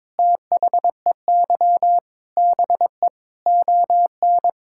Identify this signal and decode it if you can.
Morse code, heard as a keyed tone, about 22 wpm, 700 Hz. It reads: THEY BE ON
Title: Sentences from Top 1000 Words - Review Part 6 - 22wpm